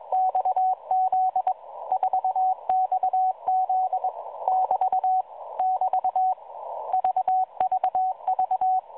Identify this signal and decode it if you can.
Morse code, heard as a keyed tone, about 21 wpm, 750 Hz, with radio noise. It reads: XZ 4XZ <BT> <BT> VVV
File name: 4XZ Shazepe